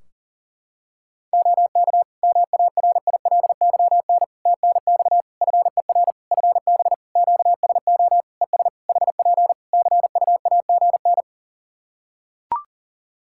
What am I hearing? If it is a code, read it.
Morse code, heard as a keyed tone, about 41 wpm, 700 Hz. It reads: OK MARILYN TNX FER FB QSO ES HP CUAGN
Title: Practice QSO - Exchange 4 - Ending - 40wpm